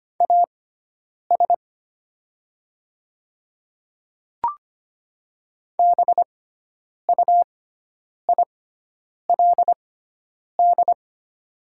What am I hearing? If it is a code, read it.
Morse code, heard as a keyed tone, about 25 wpm, spaced as for 9 wpm, 700 Hz. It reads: AS BUILD
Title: 6x Character Spacing - Top 200 Words - 25wpm